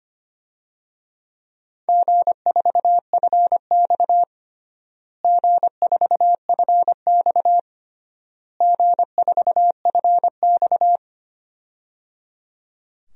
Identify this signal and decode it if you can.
Morse code, heard as a keyed tone, about 25 wpm, 700 Hz. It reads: G4FX G4FX G4FX
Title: DXCC-100 Call Signs - 4 Characters - Repeated 3x - 25wpm